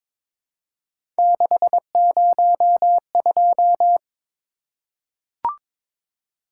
Morse code, heard as a keyed tone, about 22 wpm, 700 Hz. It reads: 602
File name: practice / 22wpm / Three Letter-Number Combinations - 22wpm